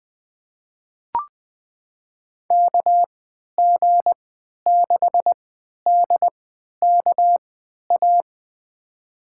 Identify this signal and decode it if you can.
Morse code, heard as a keyed tone, about 20 wpm, spaced as for 11 wpm, 700 Hz. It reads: KG6DKA
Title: USA Call Signs - 6 Characters - 3x Character Spacing - 20wpm